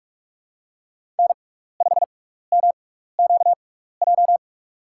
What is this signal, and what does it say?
Morse code, heard as a keyed tone, about 42 wpm, spaced as for 15 wpm, 700 Hz. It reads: N5MQJ
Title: USA Call Signs - 6x Character Spacing - 45wpm